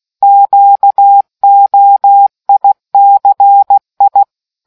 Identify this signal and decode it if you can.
Morse code, heard as a keyed tone, about 16 wpm, 800 Hz. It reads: QOICI